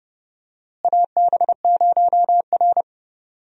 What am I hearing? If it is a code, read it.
Morse code, heard as a keyed tone, about 30 wpm, 700 Hz. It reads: AB0R
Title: USA Call Signs - 4 Characters - Repeated 2x - 30wpm